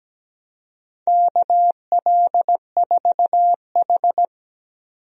Morse code, heard as a keyed tone, about 17 wpm, 700 Hz. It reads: KL4H
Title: USA Call Signs - 4 Characters - 17wpm